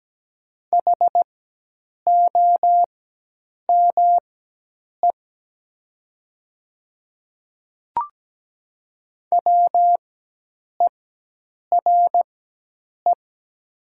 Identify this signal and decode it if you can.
Morse code, heard as a keyed tone, about 17 wpm, spaced as for 8 wpm, 700 Hz. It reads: HOME WERE
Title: 4x Character Spacing - Top 200 Words - 17wpm